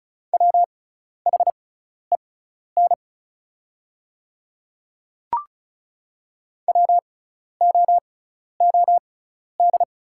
Morse code, heard as a keyed tone, about 34 wpm, spaced as for 12 wpm, 690 Hz. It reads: WHEN WOOD
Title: 6x Character Spacing - Top 500 Words - 35wpm